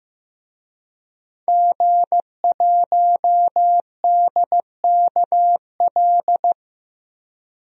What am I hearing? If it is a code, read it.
Morse code, heard as a keyed tone, about 15 wpm, 700 Hz. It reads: G1DKL
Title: DXCC-100 Call Signs - 5 Characters - 15wpm